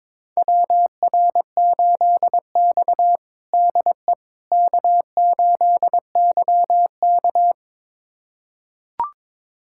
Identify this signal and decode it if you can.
Morse code, heard as a keyed tone, about 21 wpm, 700 Hz. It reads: WR8X DE K8YK
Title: USA Callsign DE Callsign - 4 Characters - 22wpm